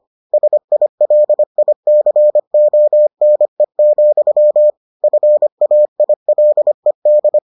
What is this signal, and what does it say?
Morse code, heard as a keyed tone, about 25 wpm, 600 Hz. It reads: SILICONE, FAILED